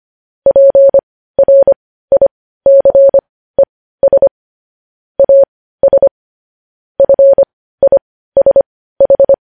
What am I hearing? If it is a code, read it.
Morse code, heard as a keyed tone, about 25 wpm, spaced as for 15 wpm, 550 Hz. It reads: PRICES AS FISH
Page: news-headlines-in-morse-code-at-15-wpm